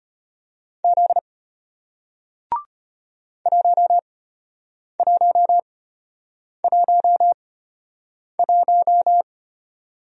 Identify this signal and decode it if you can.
Morse code, heard as a keyed tone, about 34 wpm, spaced as for 8 wpm, 700 Hz. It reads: Z 1111